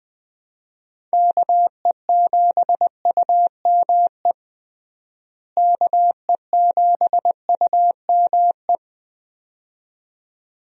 Morse code, heard as a keyed tone, about 20 wpm, 700 Hz. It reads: KE7UME KE7UME
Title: USA Call Signs - 6 Characters - Repeated 2x - 20wpm